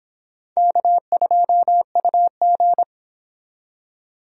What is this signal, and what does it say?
Morse code, heard as a keyed tone, about 30 wpm, 700 Hz. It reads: K2UG